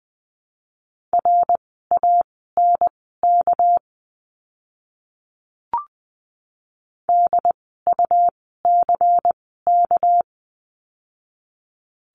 Morse code, heard as a keyed tone, about 20 wpm, spaced as for 14 wpm, 700 Hz. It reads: RANK DUCK
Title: 199 Lesson 32f - Letter k - Words 20wpm